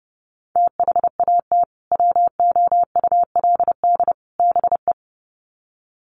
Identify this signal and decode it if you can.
Morse code, heard as a keyed tone, about 30 wpm, 700 Hz. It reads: THAT WOULD BE
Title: Sets of 3 Words 30wpm